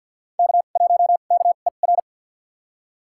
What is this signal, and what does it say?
Morse code, heard as a keyed tone, about 53 wpm, spaced as for 37 wpm, 700 Hz. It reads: K1KER
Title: USA Call Signs - 2x Character Spacing - 50wpm